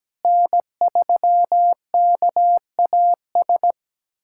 Morse code, heard as a keyed tone, about 17 wpm, 700 Hz. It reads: N3KAS
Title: USA Callsign DE Callsign 17wpm